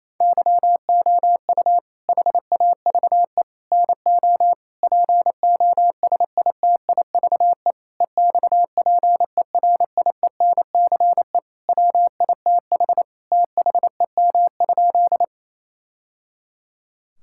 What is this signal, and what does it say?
Morse code, heard as a keyed tone, about 28 wpm, 700 Hz. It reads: YOU HAVE NO POSITIVE EXPERIENCE WITH THEM?
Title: Sentences from Top 1000 Words - Review Part 5 - 28wpm